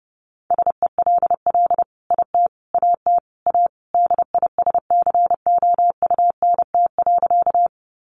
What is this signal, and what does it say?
Morse code, heard as a keyed tone, about 30 wpm, 700 Hz. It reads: SELL IT AT A DISCOUNT.
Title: Sentences from Top 1200 Words 30wpm